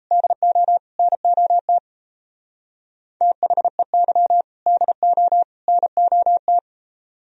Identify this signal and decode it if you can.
Morse code, heard as a keyed tone, about 35 wpm, 700 Hz. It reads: DO NOT THEY DO NOT
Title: Speed-Racing - Sets of 3 Words 28wpm